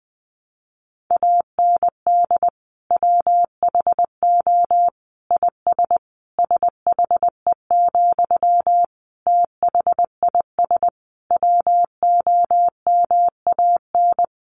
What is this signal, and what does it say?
Morse code, heard as a keyed tone, about 20 wpm, 700 Hz. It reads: AND WHO IS SHE, THIS WOMAN
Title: Sentences from Top 1100 Words 20wpm